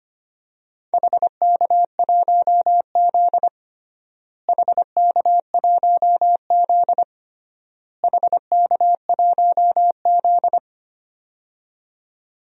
Morse code, heard as a keyed tone, about 25 wpm, 700 Hz. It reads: HK1Z HK1Z HK1Z
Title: DXCC-100 Call Signs - 4 Characters - Repeated 3x - 25wpm